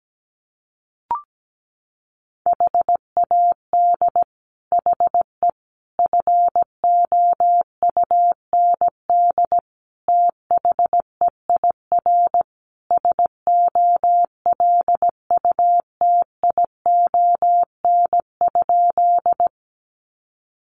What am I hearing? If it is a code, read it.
Morse code, heard as a keyed tone, about 17 wpm, 700 Hz. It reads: HAD HE FOUND THEIR SOLUTION?
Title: Sentences from Top 1300 Words 17wpm